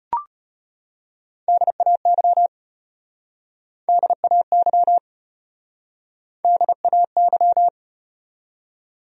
Morse code, heard as a keyed tone, about 33 wpm, 700 Hz. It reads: DAY DAY DAY